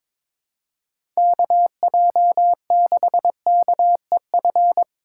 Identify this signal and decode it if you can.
Morse code, heard as a keyed tone, about 22 wpm, 700 Hz. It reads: KJ6KEF
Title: USA Call Signs - 6 Characters - 4x Character Spacing - 22wpm